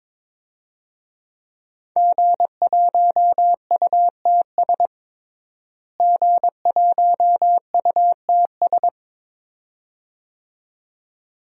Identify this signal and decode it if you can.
Morse code, heard as a keyed tone, about 22 wpm, 700 Hz. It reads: G1UTS G1UTS
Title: DXCC-100 Call Signs - 5 Characters - Repeated 2x - 22wpm